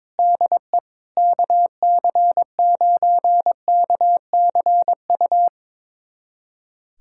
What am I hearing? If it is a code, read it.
Morse code, heard as a keyed tone, about 22 wpm, 700 Hz. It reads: DE KC9KCU
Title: USA Callsign DE Callsign 22wpm